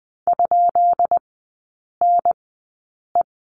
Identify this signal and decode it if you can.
Morse code, heard as a keyed tone, about 20 wpm, spaced as for 8 wpm, 700 Hz. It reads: ?NE
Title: 133 Lesson 22e - Question Mark - Three Characters 20wpm